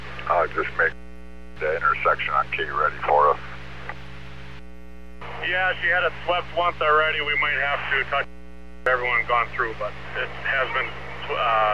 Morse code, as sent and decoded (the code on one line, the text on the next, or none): none